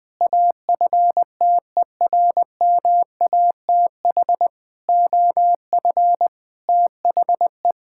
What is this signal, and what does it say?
Morse code, heard as a keyed tone, about 20 wpm, 700 Hz. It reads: AFTERMATH OF THE
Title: Sentences with 5-Word Phrases - 20wpm